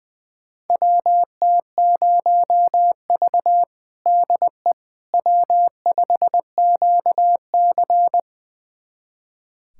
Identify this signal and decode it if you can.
Morse code, heard as a keyed tone, about 20 wpm, 700 Hz. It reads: WT0V DE W5QC
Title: USA Callsign DE Callsign - 4 Characters - 20wpm